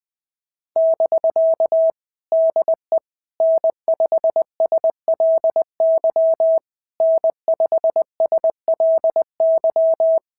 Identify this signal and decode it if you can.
Morse code, heard as a keyed tone, about 20 wpm, 650 Hz. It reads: <BK> DE N5SLY N5SLY
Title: POTA QSO Practice - 20wpm